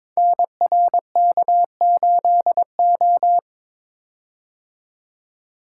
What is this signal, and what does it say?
Morse code, heard as a keyed tone, about 23 wpm, 700 Hz. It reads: NRK8O
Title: Code Groups - 5 Letter:Numbers - 22wpm